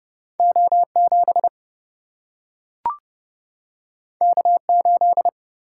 Morse code, heard as a keyed tone, about 30 wpm, 700 Hz. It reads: O7 K8
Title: Single-Letter Single-Number 30wpm